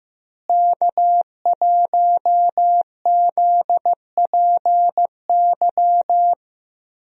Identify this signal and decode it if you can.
Morse code, heard as a keyed tone, about 16 wpm, 700 Hz. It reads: K1ZPY
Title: USA Call Signs - 5 Characters - Repeated 3x - 15wpm